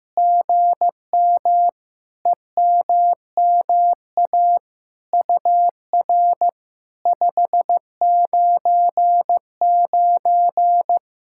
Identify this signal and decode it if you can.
Morse code, heard as a keyed tone, about 15 wpm, 700 Hz. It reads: GM EMMA UR 599
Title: SOTA QSO Practice 15wpm